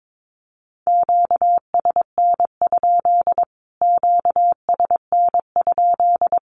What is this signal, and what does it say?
Morse code, heard as a keyed tone, about 22 wpm, 700 Hz. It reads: QSN? QSN?
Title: Q-Codes Repeated 3x 22wpm